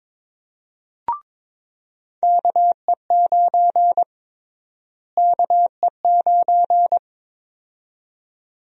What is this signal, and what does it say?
Morse code, heard as a keyed tone, about 22 wpm, 700 Hz. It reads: KE9 KE9
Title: USA Call Sign Prefixes - Repeated 2x - 22wpm